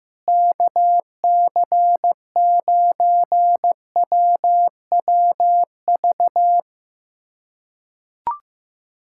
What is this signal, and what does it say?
Morse code, heard as a keyed tone, about 15 wpm, 700 Hz. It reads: KC9WWV
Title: USA Callsign DE Callsign 15wpm